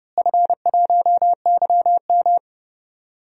Morse code, heard as a keyed tone, about 30 wpm, 700 Hz. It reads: F1YM
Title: DXCC-100 Call Signs - 4 Characters - Repeated 3x - 30wpm